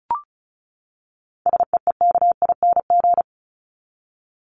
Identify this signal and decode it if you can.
Morse code, heard as a keyed tone, about 34 wpm, 700 Hz. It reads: SEEKING
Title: ING Suffix - Top 100 Words with Base Words 35wpm